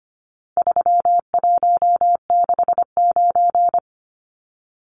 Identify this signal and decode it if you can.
Morse code, heard as a keyed tone, about 25 wpm, 700 Hz. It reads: 3169
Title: Numbers - 4-Digit 25wpm